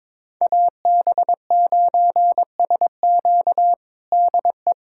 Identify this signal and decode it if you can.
Morse code, heard as a keyed tone, about 22 wpm, 700 Hz. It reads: AB9SQ DE
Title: USA Callsign DE Callsign 22wpm